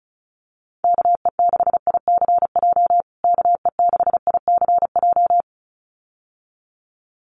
Morse code, heard as a keyed tone, about 34 wpm, 700 Hz. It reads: KE6ICJ KE6ICJ
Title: USA Call Signs Repeated 2x 35wpm